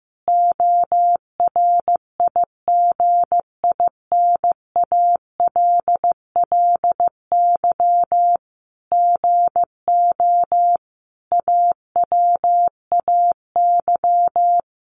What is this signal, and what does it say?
Morse code, heard as a keyed tone, about 15 wpm, 700 Hz. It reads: ORIGINALLY GO AWAY
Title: Sentences from Top 1500 Words 15wpm